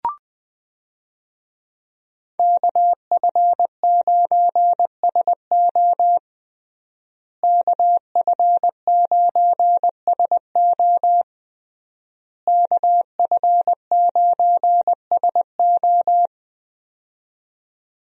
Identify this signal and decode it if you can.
Morse code, heard as a keyed tone, about 20 wpm, 700 Hz. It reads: KF9SO KF9SO KF9SO
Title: USA Call Signs - 5 Characters - Repeated 3x - 20wpm